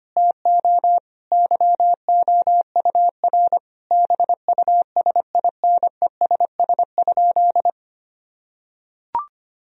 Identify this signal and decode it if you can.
Morse code, heard as a keyed tone, about 25 wpm, 700 Hz. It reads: TO YOUR BUSINESS?
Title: Sentences from Top 1000 Words - Review Part 4 - 25wpm